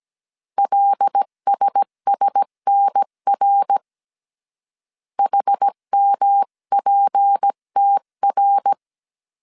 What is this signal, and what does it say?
Morse code, heard as a keyed tone, about 17 wpm, 800 Hz. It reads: LSSNR HMPTR
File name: CW / DET / 1 6 1500C S C 16wpm 2733